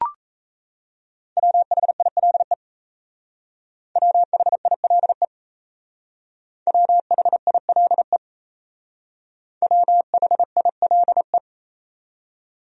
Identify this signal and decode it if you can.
Morse code, heard as a keyed tone, about 34 wpm, 700 Hz. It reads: WHILE WHILE WHILE WHILE
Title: Speed-Racing Top 300 Words 28wpm